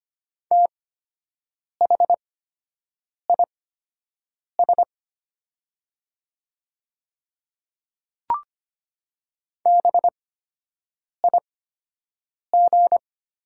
Morse code, heard as a keyed tone, about 25 wpm, spaced as for 7 wpm, 700 Hz. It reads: THIS BIG
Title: 8x Character Spacing - Top 200 Words - 25wpm